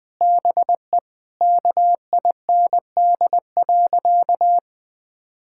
Morse code, heard as a keyed tone, about 20 wpm, 700 Hz. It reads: BE KIND.
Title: Sentences with 4-Word Phrases - Spoken Beginning - 20wpm